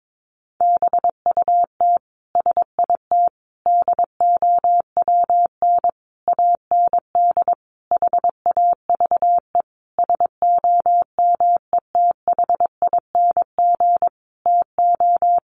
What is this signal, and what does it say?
Morse code, heard as a keyed tone, about 22 wpm, 700 Hz. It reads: BUT SIT DOWN AND HAVE SOMETHING TO
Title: Sentences from Top 600 Words 22wpm